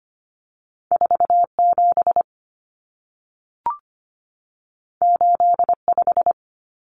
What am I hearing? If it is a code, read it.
Morse code, heard as a keyed tone, about 25 wpm, 700 Hz. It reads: 47 85
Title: Numbers - 2-Digit 25wpm